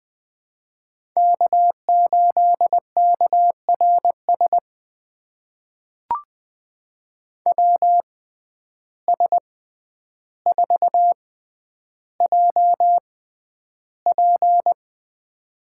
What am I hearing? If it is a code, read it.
Morse code, heard as a keyed tone, about 20 wpm, spaced as for 10 wpm, 700 Hz. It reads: K8KRS WS4JP
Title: USA Call Signs - 5 Characters - 6x Character Spacing - 20wpm